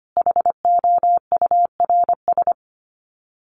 Morse code, heard as a keyed tone, about 25 wpm, 700 Hz. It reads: HOURS
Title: Sets of 5 Words Encore2 25wpm